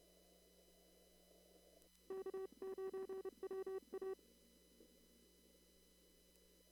Morse code, heard as a keyed tone, about 29 wpm, 380 Hz. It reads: K9WA
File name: CW / kt5j wpx cw-op-k5pi